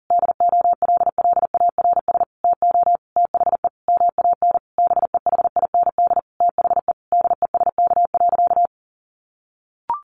Morse code, heard as a keyed tone, about 37 wpm, 700 Hz. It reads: DOLLARS TO THE MAN BEHIND THE DESK.
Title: A Sound Of Thunder 40wpm